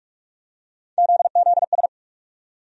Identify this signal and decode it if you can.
Morse code, heard as a keyed tone, about 46 wpm, 700 Hz. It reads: ZZS
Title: USA Call Sign Postfixes - 3 Letters 45wpm